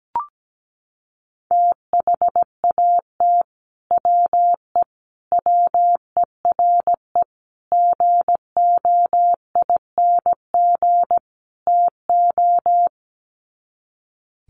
Morse code, heard as a keyed tone, about 17 wpm, 700 Hz. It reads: THAT WE WERE GOING TO
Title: Sets of 5 Words Encore2 17wpm